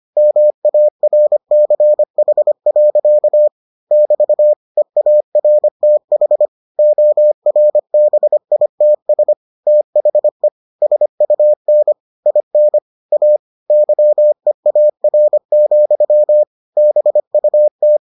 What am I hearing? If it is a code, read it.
Morse code, heard as a keyed tone, about 25 wpm, 600 Hz. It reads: MARCH. <BT> EARTH ORBITS THE SUN IN A YEAR, BUT